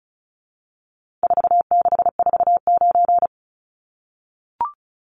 Morse code, heard as a keyed tone, about 34 wpm, 700 Hz. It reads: 4649
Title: Numbers - 4-Digit 35wpm